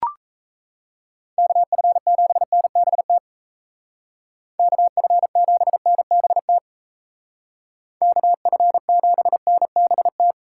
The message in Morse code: KF7NBT KF7NBT KF7NBT

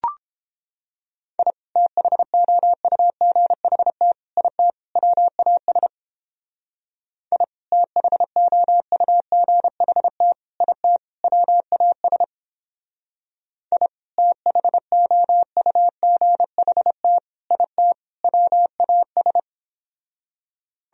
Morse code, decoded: I THOUGHT IT WAS I THOUGHT IT WAS I THOUGHT IT WAS

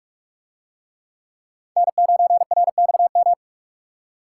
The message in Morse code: N9RXM

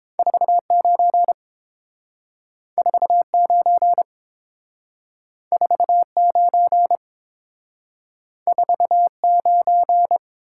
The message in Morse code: 49 49 49 49

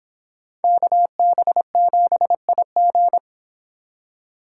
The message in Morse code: KB7IG

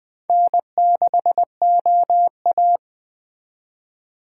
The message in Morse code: N6OA

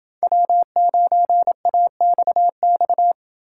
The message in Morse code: W9AXX

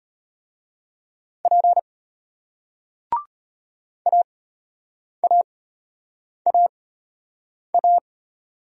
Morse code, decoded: P AAAA

